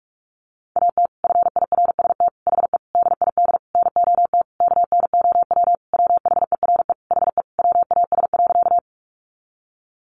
Morse code, heard as a keyed tone, about 46 wpm, 700 Hz. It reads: AT FIRST HE DID NOT KNOW WHERE HE WAS.
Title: Sentences from Top 600 Words 45wpm